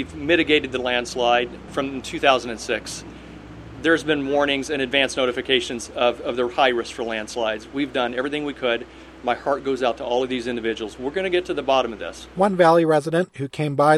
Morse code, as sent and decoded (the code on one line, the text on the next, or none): none